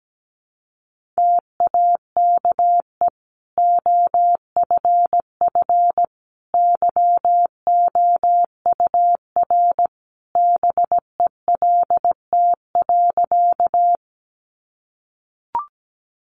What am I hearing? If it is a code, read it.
Morse code, heard as a keyed tone, about 17 wpm, 700 Hz. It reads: TAKE OFF YOUR BELT.
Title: Sentences from Top 1400 Words 17wpm